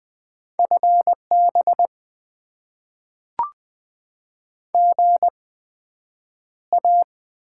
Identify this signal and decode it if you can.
Morse code, heard as a keyed tone, about 20 wpm, spaced as for 9 wpm, 700 Hz. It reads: FB GA
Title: Top 20 QSO Elements - 8x Character Spacing - 20wpm